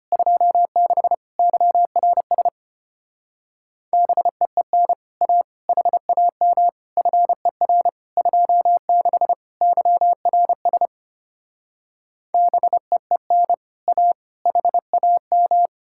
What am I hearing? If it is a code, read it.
Morse code, heard as a keyed tone, about 33 wpm, 700 Hz. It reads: 26 YRS BEEN A HAM FER 26 YRS BEEN A HAM